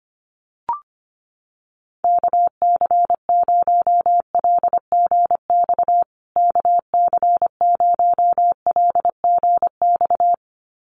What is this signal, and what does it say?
Morse code, heard as a keyed tone, about 25 wpm, 700 Hz. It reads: KC0LGX KC0LGX
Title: USA Call Signs Repeated 2x 25wpm